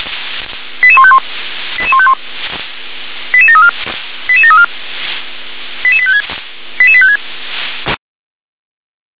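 Five-tone selective call (ZVEI), heard as quick runs of five tones, tones of about 70 ms. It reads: 88151 88151 88734 88734 88856 88856
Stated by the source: ZVEI five-tone call 88509 received by SDR.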